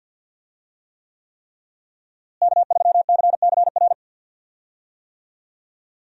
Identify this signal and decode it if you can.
Morse code, heard as a keyed tone, about 52 wpm, 700 Hz. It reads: K3CCR
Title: USA Call Signs - 5 Characters - Repeated 3x - 50wpm